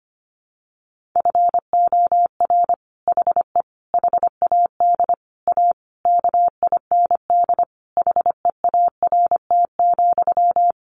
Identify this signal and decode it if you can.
Morse code, heard as a keyed tone, about 25 wpm, 700 Hz. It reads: FOR HE HAD A KIND HEART,
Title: Sentences from Top 1400 Words 25wpm